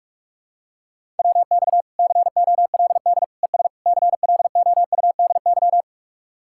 Morse code, heard as a keyed tone, about 38 wpm, 700 Hz. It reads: WX COLD ES CLOUDY
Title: Speed-Racing Common QSO Phrases - 30wpm